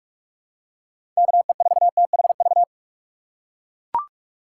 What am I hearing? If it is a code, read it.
Morse code, heard as a keyed tone, about 42 wpm, 700 Hz. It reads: KE4THV
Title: USA Call Signs - 6 Characters - 3x Character Spacing - 45wpm